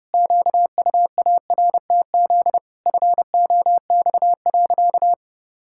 Morse code, heard as a keyed tone, about 30 wpm, 700 Hz. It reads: QUARTZ FOX.
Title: Pangram Challenge v2 - 30wpm